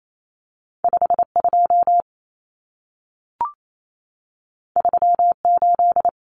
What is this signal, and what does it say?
Morse code, heard as a keyed tone, about 28 wpm, 700 Hz. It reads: 52 38
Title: Numbers - 2-Digit 28wpm